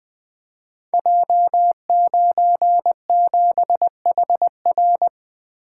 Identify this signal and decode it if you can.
Morse code, heard as a keyed tone, about 20 wpm, 700 Hz. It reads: J97HR
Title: Code Groups - 5 Letter:Numbers - 20wpm